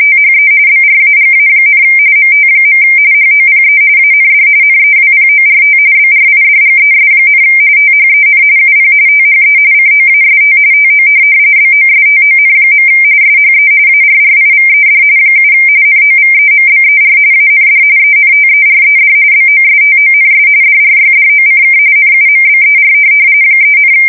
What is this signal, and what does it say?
RTTY, about 45.45 baud. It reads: THE BRAGA, MT. HOPE AND NEWPORT BRIDGES, THEN UP THE COAST TO BOSTON. SHE ARRIVED 7 NOVEMBER AND ENTERED BOSTONS DRYDOCK NUMBER 3,